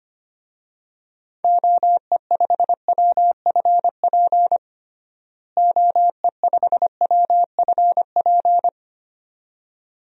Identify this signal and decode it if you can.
Morse code, heard as a keyed tone, about 25 wpm, 700 Hz. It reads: OE5WFP OE5WFP
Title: DXCC-100 Call Signs - 6 Characters - Repeated 2x - 25wpm